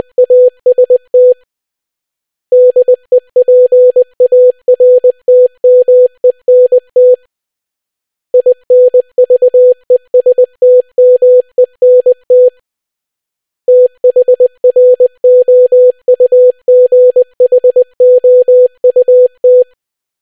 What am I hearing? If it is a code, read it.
Morse code, heard as a keyed tone, about 20 wpm, 500 Hz. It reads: AST DEPARTMENT INVESTMENT THROUGHOUT